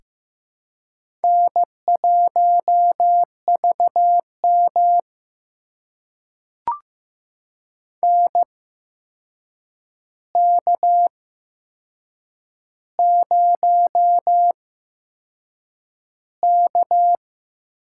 Morse code, heard as a keyed tone, about 15 wpm, spaced as for 6 wpm, 700 Hz. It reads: N1VM NK0K
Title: USA Call Signs - 4 Characters - 8x Character Spacing - 15wpm